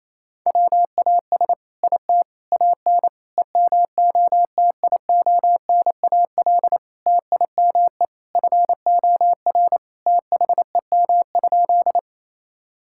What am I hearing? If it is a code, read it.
Morse code, heard as a keyed tone, about 28 wpm, 700 Hz. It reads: WAS IT AN EMOTIONAL TIME FOR THEM?
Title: Sentences from Top 1000 Words - Review Part 5 - 28wpm